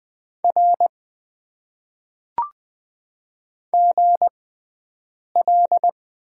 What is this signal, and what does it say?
Morse code, heard as a keyed tone, about 20 wpm, spaced as for 7 wpm, 700 Hz. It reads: R GL
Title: Top 90 QSO Elements - 6x Character Spacing - 20wpm